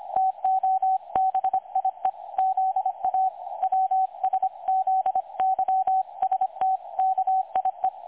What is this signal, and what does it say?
Morse code, heard as a keyed tone, about 25 wpm, 760 Hz, with radio noise. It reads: TOBIE ZA WSZYSTKIE
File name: dlazaawansowanych / nagraniazpasma / websdr recording 3565.0kHz